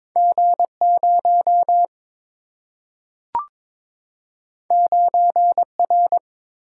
Morse code, heard as a keyed tone, about 22 wpm, 700 Hz. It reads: G0 9R
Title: Single-Letter Single-Number 22wpm